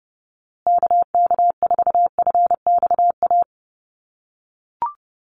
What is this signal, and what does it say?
Morse code, heard as a keyed tone, about 30 wpm, 700 Hz. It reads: KK4FXA
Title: USA Call Signs Repeated 2x 30wpm